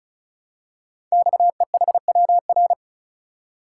Morse code, heard as a keyed tone, about 36 wpm, 700 Hz. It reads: XEHWR
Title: Code Groups - 5 Letters - 35wpm